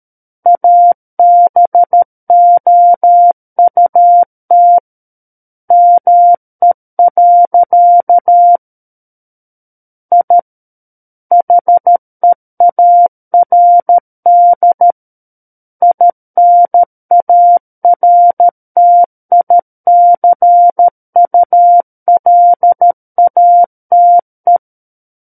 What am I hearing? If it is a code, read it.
Morse code, heard as a keyed tone, about 13 wpm, 700 Hz. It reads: ABOUT ME. I HEARD INARTICULATE
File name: Book One / 04-Chapter 4 - 13 WPM